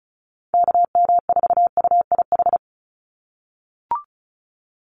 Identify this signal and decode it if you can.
Morse code, heard as a keyed tone, about 36 wpm, 700 Hz. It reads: KM4UIH
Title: USA Call Signs 35wpm